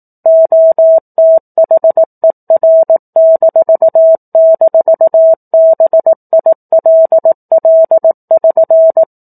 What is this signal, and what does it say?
Morse code, heard as a keyed tone, about 18 wpm, 650 Hz. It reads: OTHER--BILL<SN>